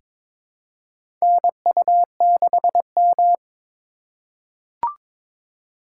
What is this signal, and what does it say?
Morse code, heard as a keyed tone, about 22 wpm, 700 Hz. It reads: NU6M
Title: USA Call Signs - 2x Character Spacing - 22wpm